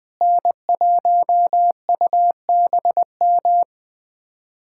Morse code, heard as a keyed tone, about 20 wpm, 700 Hz. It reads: N1UBM
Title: USA Callsign DE Callsign - 5 Characters - 20wpm